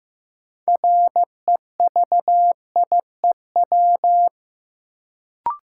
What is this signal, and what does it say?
Morse code, heard as a keyed tone, about 15 wpm, 700 Hz. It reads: REVIEW
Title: Sentences from Top 900 Words - New Word Drill - 15wpm